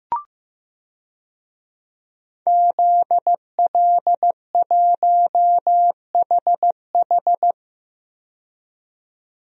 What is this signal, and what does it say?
Morse code, heard as a keyed tone, about 15 wpm, 700 Hz. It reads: ZL1HH
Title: DXCC-100 Call Signs - 5 Characters - 15wpm